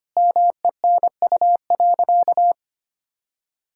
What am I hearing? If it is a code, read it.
Morse code, heard as a keyed tone, about 25 wpm, 700 Hz. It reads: MENU.
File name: practice / 25wpm / Sentences with 4-Word Phrases - 25wpm